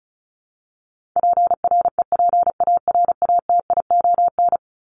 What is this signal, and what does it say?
Morse code, heard as a keyed tone, about 34 wpm, 700 Hz. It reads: PREPARATION
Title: Top 1251-1500 Words 35wpm